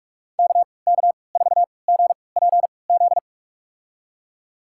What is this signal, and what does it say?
Morse code, heard as a keyed tone, about 46 wpm, spaced as for 25 wpm, 700 Hz. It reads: KK4GPZ
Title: USA Call Signs - 6 Characters - 3x Character Spacing - 45wpm